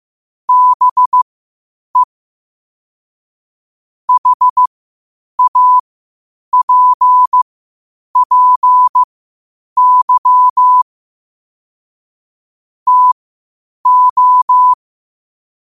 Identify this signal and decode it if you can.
Morse code, heard as a keyed tone, about 15 wpm, spaced as for 8 wpm, 1000 Hz. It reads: BE HAPPY TO
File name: qotdfiles / 8wpm